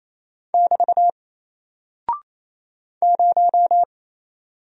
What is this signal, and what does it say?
Morse code, heard as a keyed tone, about 27 wpm, 700 Hz. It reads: <BT> 0
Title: Single Character (Letter:Number:Punctuation:Prosign) Alt 28wpm